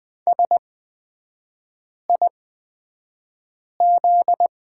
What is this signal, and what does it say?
Morse code, heard as a keyed tone, about 20 wpm, spaced as for 5 wpm, 700 Hz. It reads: SIZ
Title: 243 Lesson 39e - Letter z - Three Characters 20wpm